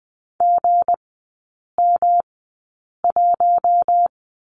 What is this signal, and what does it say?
Morse code, heard as a keyed tone, about 20 wpm, spaced as for 8 wpm, 700 Hz. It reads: GM1
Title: 172 Lesson 28e - Number 9 - Three Characters 20wpm